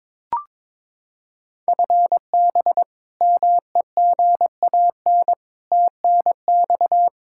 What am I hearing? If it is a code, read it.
Morse code, heard as a keyed tone, about 22 wpm, 700 Hz. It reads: FB MEGAN TNX
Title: Practice QSO - Exchange 3 - More Info - 22wpm